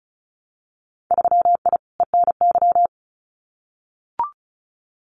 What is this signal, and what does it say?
Morse code, heard as a keyed tone, about 34 wpm, 700 Hz. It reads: 3I ENY
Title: Winter Field Day Exchanges Encore 35wpm